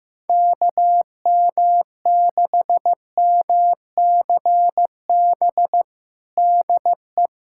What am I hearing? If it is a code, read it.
Morse code, heard as a keyed tone, about 15 wpm, 700 Hz. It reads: KM6MCB DE
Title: USA Callsign DE Callsign 15wpm